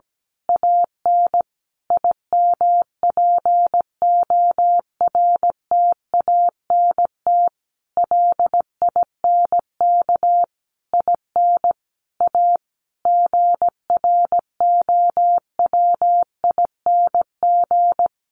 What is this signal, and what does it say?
Morse code, heard as a keyed tone, about 17 wpm, 700 Hz. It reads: AN IMPORTANT LINK IN A GROWING
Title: A Sound Of Thunder 17wpm